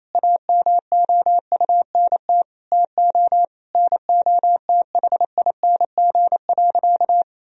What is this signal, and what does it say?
Morse code, heard as a keyed tone, about 28 wpm, 700 Hz. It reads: AMOUNT TO NOTHING.
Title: Sentences from Top 1000 Words - Review Part 4 - 28wpm